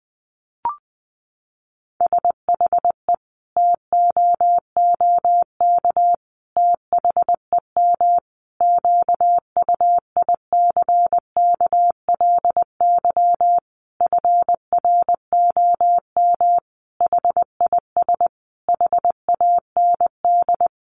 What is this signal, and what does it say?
Morse code, heard as a keyed tone, about 20 wpm, 700 Hz. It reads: SHE TOOK THEM QUICKLY FROM HIS HAND
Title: Sentences from Top 1300 Words 20wpm